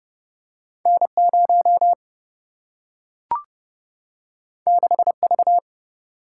Morse code, heard as a keyed tone, about 30 wpm, 700 Hz. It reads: N 0 6 V
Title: Single-Letter Single-Number 30wpm